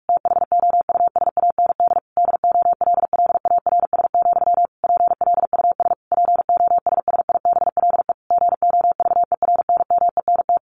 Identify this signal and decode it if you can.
Morse code, heard as a keyed tone, about 44 wpm, 700 Hz. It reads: THOUSAND DOLLARS, PLUS POSSIBLE GOVERNMENT
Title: A Sound Of Thunder 45wpm